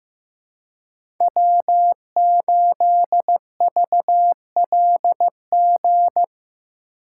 Morse code, heard as a keyed tone, about 15 wpm, 700 Hz. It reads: W8VLG
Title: USA Call Signs - 5 Characters - 4x Character Spacing - 15wpm